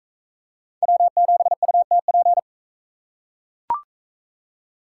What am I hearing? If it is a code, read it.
Morse code, heard as a keyed tone, about 38 wpm, 700 Hz. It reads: W7UTP